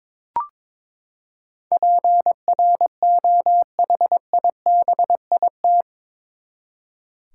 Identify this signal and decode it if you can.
Morse code, heard as a keyed tone, about 22 wpm, 700 Hz. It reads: PROHIBIT
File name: practice / 22wpm / Eight Letter Words - 22wpm